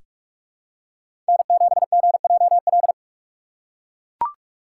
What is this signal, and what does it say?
Morse code, heard as a keyed tone, about 48 wpm, 700 Hz. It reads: N7GJL
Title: USA Call Signs - 5 Characters - 8x Character Spacing - 45wpm